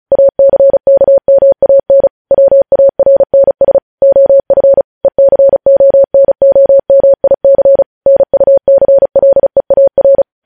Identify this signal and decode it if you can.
Morse code, heard as a keyed tone, about 36 wpm, 550 Hz. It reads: ACKMAN WARNS OF ECONOMIC NUCLEAR
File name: podcast / News headlines-1x-35x35